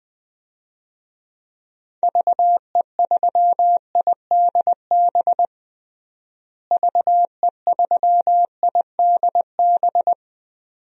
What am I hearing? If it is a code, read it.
Morse code, heard as a keyed tone, about 20 wpm, 700 Hz. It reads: VE3IDB VE3IDB
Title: DXCC-100 Call Signs - 6 Characters - Repeated 2x - 20wpm